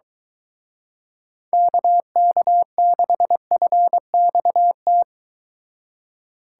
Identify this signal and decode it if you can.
Morse code, heard as a keyed tone, about 24 wpm, 700 Hz. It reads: KK6FXT